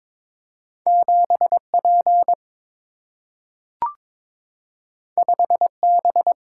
Morse code, heard as a keyed tone, about 22 wpm, 700 Hz. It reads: 7P 5B
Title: Single-Letter Single-Number 22wpm